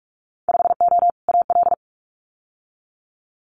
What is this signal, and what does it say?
Morse code, heard as a keyed tone, about 46 wpm, 700 Hz. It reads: 5O AL
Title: Winter Field Day Exchanges 45wpm